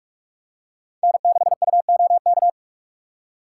Morse code, heard as a keyed tone, about 44 wpm, 700 Hz. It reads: N6UOK